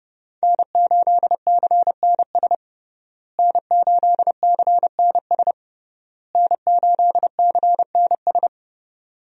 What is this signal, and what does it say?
Morse code, heard as a keyed tone, about 30 wpm, 700 Hz. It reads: N8CNS N8CNS N8CNS
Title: USA Call Signs - 5 Characters - Repeated 3x - 30wpm